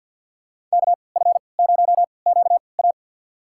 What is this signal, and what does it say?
Morse code, heard as a keyed tone, about 53 wpm, spaced as for 29 wpm, 700 Hz. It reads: KF0QA
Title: USA Call Signs - 5 Characters - 3x Character Spacing - 50wpm